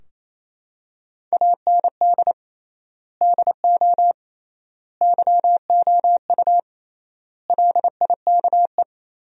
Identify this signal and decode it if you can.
Morse code, heard as a keyed tone, about 27 wpm, 700 Hz. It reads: AND DO YOU LIKE
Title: 3x Word-Spacing - Sentences from Top 100 Words - 28wpm